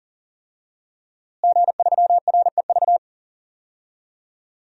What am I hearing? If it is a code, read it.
Morse code, heard as a keyed tone, about 37 wpm, 700 Hz. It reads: G3REV
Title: DXCC-100 Call Signs - 5 Characters - Repeated 3x - 40wpm